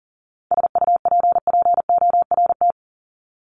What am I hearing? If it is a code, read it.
Morse code, heard as a keyed tone, about 40 wpm, 700 Hz. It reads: SUPPORT
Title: Top 751-1000 Words 40wpm